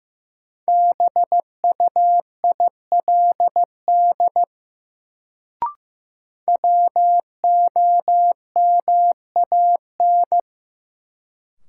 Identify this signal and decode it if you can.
Morse code, heard as a keyed tone, about 15 wpm, 700 Hz. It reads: BUILD WOMAN
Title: Sentences from Top 1100 Words - New Word Drill Alt - 15wpm